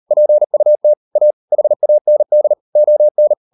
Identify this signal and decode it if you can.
Morse code, heard as a keyed tone, about 38 wpm, 600 Hz. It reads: PUT A HAND ON